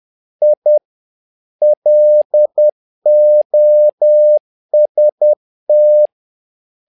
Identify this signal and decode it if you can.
Morse code, heard as a keyed tone, about 10 wpm, 600 Hz. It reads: I LOST
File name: learning center / tower trouble / tower trouble 10wpm